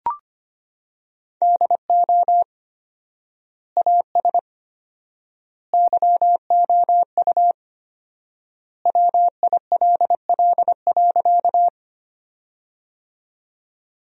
Morse code, heard as a keyed tone, about 25 wpm, 700 Hz. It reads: DO AS YOU WILL.
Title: 4x Word-Spacing - Sentences from Top 100 Words - 25wpm